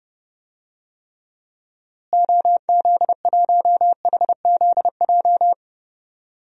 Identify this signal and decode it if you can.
Morse code, heard as a keyed tone, about 30 wpm, 700 Hz. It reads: OZ1HZJ
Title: DXCC-100 Call Signs - 6 Characters - Repeated 3x - 30wpm